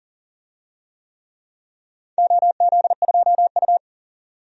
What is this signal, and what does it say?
Morse code, heard as a keyed tone, about 40 wpm, 700 Hz. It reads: OZ2U
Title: DXCC-100 Call Signs - 4 Characters - 40wpm